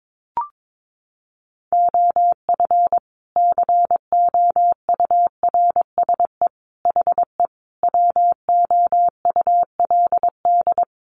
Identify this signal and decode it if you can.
Morse code, heard as a keyed tone, about 22 wpm, 700 Hz. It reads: OF COURSE HE WOULD
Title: Sentences from Top 1200 Words 22wpm